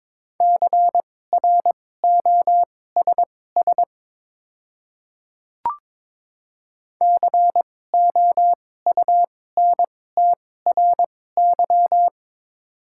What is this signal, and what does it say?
Morse code, heard as a keyed tone, about 22 wpm, spaced as for 16 wpm, 700 Hz. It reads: CROSS COUNTRY
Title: 2x Character Spacing - Top 300 Words - 22wpm